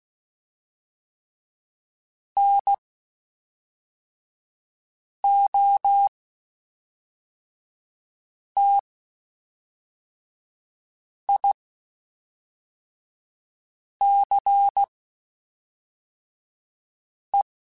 Morse code, heard as a keyed tone, about 16 wpm, spaced as for 3 wpm, 800 Hz. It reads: NOTICE
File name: (spaced) / QSOs Spaced-mp3s / WD0BGP 1 @5 sp